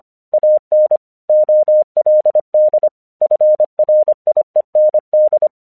Morse code, heard as a keyed tone, about 25 wpm, 610 Hz. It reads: AN OLD FRIEND
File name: 1 English 25wpm / AQC32